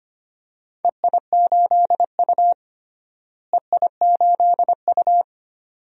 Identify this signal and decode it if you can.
Morse code, heard as a keyed tone, about 25 wpm, 700 Hz. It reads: EI8U EI8U
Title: DXCC-100 Call Signs - 4 Characters - Repeated 2x - 25wpm